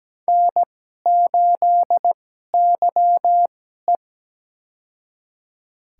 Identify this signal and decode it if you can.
Morse code, heard as a keyed tone, about 17 wpm, spaced as for 12 wpm, 700 Hz. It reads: N8YE
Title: USA Call Signs - 4 Characters - 2x Character Spacing - 17wpm